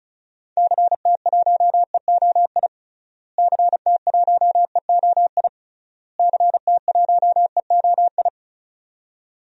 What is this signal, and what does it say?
Morse code, heard as a keyed tone, about 36 wpm, 700 Hz. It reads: CT1EOI CT1EOI CT1EOI
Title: DXCC-100 Call Signs - 6 Characters - Repeated 3x - 35wpm